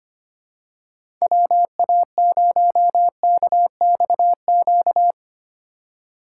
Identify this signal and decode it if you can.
Morse code, heard as a keyed tone, about 25 wpm, 700 Hz. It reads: WA0KXQ
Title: USA Call Signs - 6 Characters - 4x Character Spacing - 25wpm